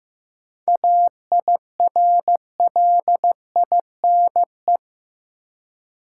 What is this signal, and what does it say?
Morse code, heard as a keyed tone, about 16 wpm, 700 Hz. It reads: AIRLINE
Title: Seven Letter Words - 15wpm